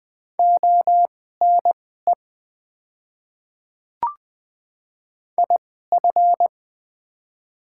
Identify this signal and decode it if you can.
Morse code, heard as a keyed tone, about 20 wpm, spaced as for 14 wpm, 690 Hz. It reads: ONE IF
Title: 2x Character Spacing - Top 100 Words - 20wpm